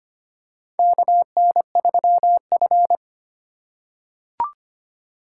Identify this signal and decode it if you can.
Morse code, heard as a keyed tone, about 25 wpm, 700 Hz. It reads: KN3F
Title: USA Call Signs - 4 Characters - 2x Character Spacing - 25wpm